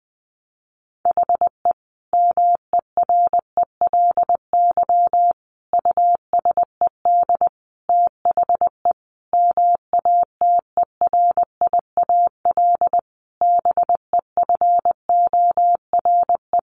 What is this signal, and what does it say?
Morse code, heard as a keyed tone, about 20 wpm, 700 Hz. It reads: HE MERELY USED THE MATERIAL BEFORE
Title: Sentences from Top 1400 Words 20wpm